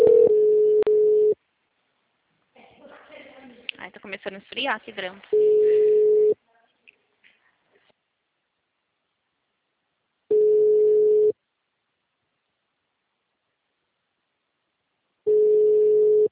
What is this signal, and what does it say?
Ringback tone (Europe, 425 Hz), with speech.
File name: out- -2027- - - .986